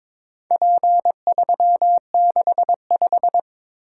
Morse code, heard as a keyed tone, about 22 wpm, 700 Hz. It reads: P365
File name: practice / 22wpm / Four Letter-Number Combinations - 22wpm